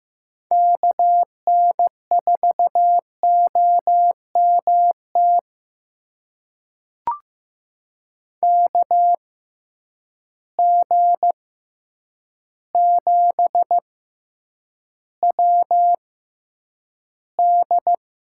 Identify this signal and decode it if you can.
Morse code, heard as a keyed tone, about 15 wpm, spaced as for 8 wpm, 700 Hz. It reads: KN4OMT KG7WD
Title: USA Call Signs - 6 Characters - 6x Character Spacing - 15wpm